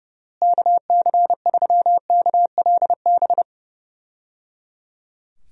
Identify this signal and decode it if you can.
Morse code, heard as a keyed tone, about 28 wpm, 700 Hz. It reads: KC3KLB
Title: USA Call Signs - 6 Characters - Repeated 3x - 30wpm